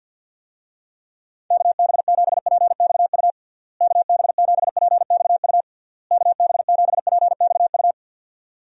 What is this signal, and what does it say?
Morse code, heard as a keyed tone, about 47 wpm, 700 Hz. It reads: KB7PXU KB7PXU KB7PXU
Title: USA Call Signs - 6 Characters - Repeated 3x - 50wpm